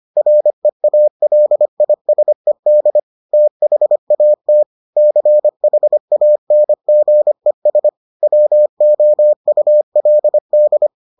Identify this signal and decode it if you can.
Morse code, heard as a keyed tone, about 25 wpm, 600 Hz. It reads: REALISED THAT CHANGES WOULD